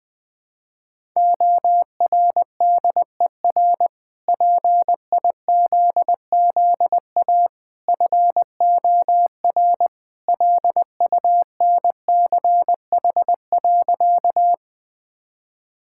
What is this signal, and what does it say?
Morse code, heard as a keyed tone, about 20 wpm, 700 Hz. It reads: ORDER PIZZA FOR LUNCH.
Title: Easy Sentences with Hints - 20wpm